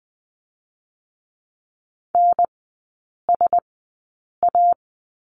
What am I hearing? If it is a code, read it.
Morse code, heard as a keyed tone, about 20 wpm, 700 Hz. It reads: N S A
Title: 035 Lesson 7e - Letter s - Three Characters 20wpm